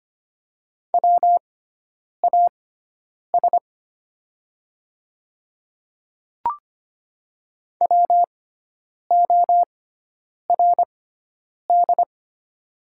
Morse code, heard as a keyed tone, about 25 wpm, spaced as for 9 wpm, 700 Hz. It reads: WAS WORD
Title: 6x Character Spacing - Top 100 Words - 25wpm